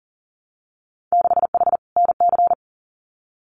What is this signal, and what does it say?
Morse code, heard as a keyed tone, about 40 wpm, 700 Hz. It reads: 6H NC
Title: Winter Field Day Exchanges Encore 40wpm